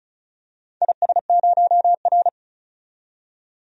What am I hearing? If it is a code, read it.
Morse code, heard as a keyed tone, about 34 wpm, 700 Hz. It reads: IS0R
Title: DXCC-100 Call Signs - 4 Characters - 35wpm